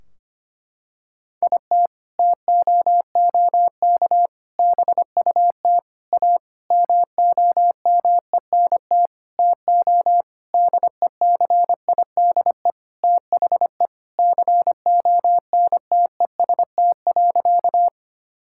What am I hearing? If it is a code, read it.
Morse code, heard as a keyed tone, about 25 wpm, 700 Hz. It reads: IT TOOK BUT A MOMENT TO DECIDE THE CONTEST.
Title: Sentences from Top 1600 Words 25wpm